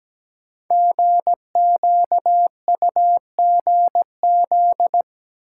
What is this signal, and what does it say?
Morse code, heard as a keyed tone, about 17 wpm, 700 Hz. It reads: GQUGZ
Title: Code Groups - 5 Letters - 17wpm